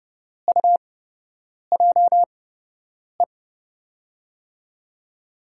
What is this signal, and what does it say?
Morse code, heard as a keyed tone, about 32 wpm, spaced as for 8 wpm, 700 Hz. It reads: UJE
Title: USA Call Signs - 6 Characters - 8x Character Spacing - 30wpm